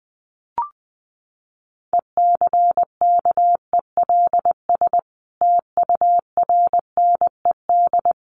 Morse code, heard as a keyed tone, about 20 wpm, 700 Hz. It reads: ECKELS TURNED
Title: A Sound Of Thunder 20wpm